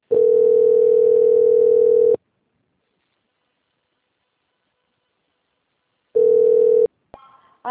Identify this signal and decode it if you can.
Ringback tone (North America), 440 + 480 Hz.
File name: out- - - - - .478